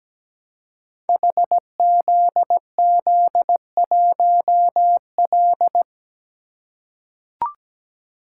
Morse code, heard as a keyed tone, about 17 wpm, 700 Hz. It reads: HZZ1L
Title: Code Groups - 5 Letter:Numbers - 17wpm